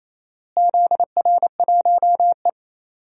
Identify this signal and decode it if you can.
Morse code, heard as a keyed tone, about 27 wpm, 700 Hz. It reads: ZR1E
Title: DXCC-100 Call Signs - 4 Characters - Repeated 3x - 28wpm